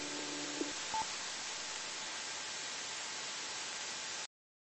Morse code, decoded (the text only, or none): none